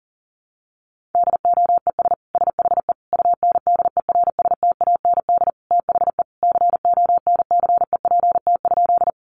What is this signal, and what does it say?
Morse code, heard as a keyed tone, about 40 wpm, 700 Hz. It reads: DOES SHE UNDERSTAND THE CONCEPT?
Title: Sentences from Top 1100 Words 40wpm